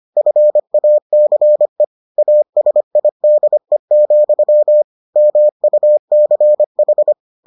Morse code, heard as a keyed tone, about 25 wpm, 600 Hz. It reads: FACE ASIDE, MUCH